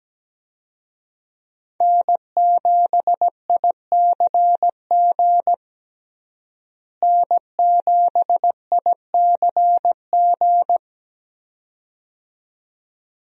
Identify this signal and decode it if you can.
Morse code, heard as a keyed tone, about 17 wpm, 700 Hz. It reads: N7ICG N7ICG
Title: USA Call Signs - 5 Characters - Repeated 2x - 17wpm